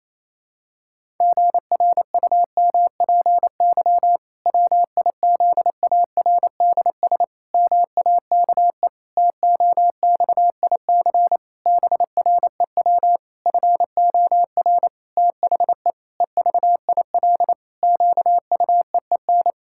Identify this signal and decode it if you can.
Morse code, heard as a keyed tone, about 28 wpm, 700 Hz. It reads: GRUMPY WIZARDS MAKE TOXIC BREW FOR THE EVIL QUEEN
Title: Pangram Challenge v2 - 28wpm